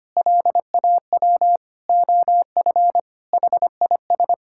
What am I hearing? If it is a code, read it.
Morse code, heard as a keyed tone, about 25 wpm, 700 Hz. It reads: LAW OF HIS
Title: Sentences from Top 1000 Words - Review Part 5 - 25wpm